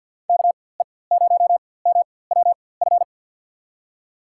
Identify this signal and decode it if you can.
Morse code, heard as a keyed tone, about 48 wpm, spaced as for 23 wpm, 700 Hz. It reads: KE0MWF
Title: USA Call Signs - 6 Characters - 4x Character Spacing - 50wpm